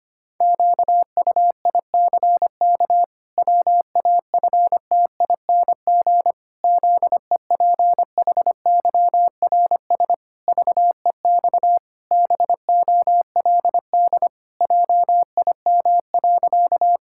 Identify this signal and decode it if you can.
Morse code, heard as a keyed tone, about 25 wpm, 700 Hz. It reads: QUICK WAFTING ZEPHYRS VEX BOLD JIM.
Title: Pangram Challenge v2 - 25wpm